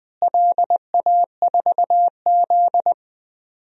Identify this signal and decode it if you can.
Morse code, heard as a keyed tone, about 20 wpm, 700 Hz. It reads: LA4Z
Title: DXCC-100 Call Signs - 4 Characters - Repeated 2x - 20wpm